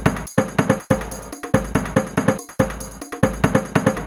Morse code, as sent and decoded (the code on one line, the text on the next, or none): none